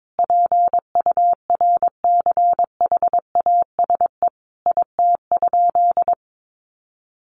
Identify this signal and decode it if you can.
Morse code, heard as a keyed tone, about 22 wpm, 700 Hz. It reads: PURCHASE IT?
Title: Sentences from Top 800 Words 22wpm